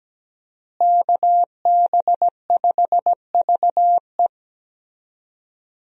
Morse code, heard as a keyed tone, about 17 wpm, 700 Hz. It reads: KB5VE
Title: USA Call Signs - 5 Characters - Repeated 3x - 17wpm